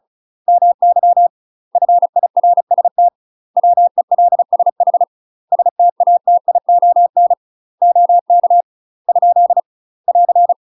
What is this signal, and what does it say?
Morse code, heard as a keyed tone, about 34 wpm, 700 Hz. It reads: MY FIRST WELSH STATION OK ? <AR>
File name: UK MASTER QSOs TRAINING / 35WPM / UK MASTER QSOs 0043